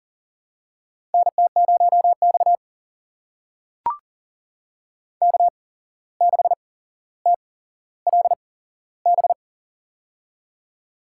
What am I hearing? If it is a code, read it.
Morse code, heard as a keyed tone, about 40 wpm, spaced as for 16 wpm, 700 Hz. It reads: NT0X K6TLB
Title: USA Call Signs - 8x Character Spacing - 40wpm